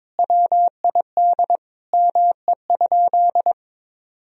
Morse code, heard as a keyed tone, about 22 wpm, 700 Hz. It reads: WID ME?
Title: Sentences from Top 400 Words and Abbreviations 22wpm